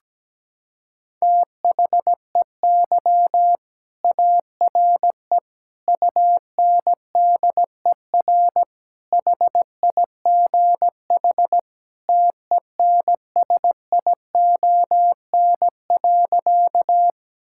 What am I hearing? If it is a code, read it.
Morse code, heard as a keyed tone, about 17 wpm, 700 Hz. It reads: THEY ARE UNDER HIGH TENSION.
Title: Sentences from Top 1600 Words 17wpm